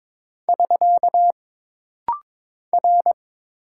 Morse code, heard as a keyed tone, about 22 wpm, 700 Hz. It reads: <SK> R
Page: This rapid-fire timing is geared to push your ICR (Instant Character Recognition) skills.